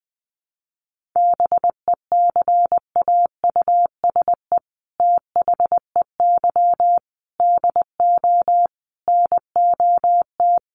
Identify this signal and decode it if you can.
Morse code, heard as a keyed tone, about 19 wpm, 700 Hz. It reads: BECAUSE THEY DO NOT
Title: Sets of 5 Words Encore2 20wpm